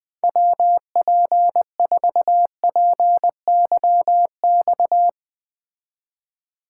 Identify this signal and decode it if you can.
Morse code, heard as a keyed tone, about 20 wpm, 700 Hz. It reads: WP4PYX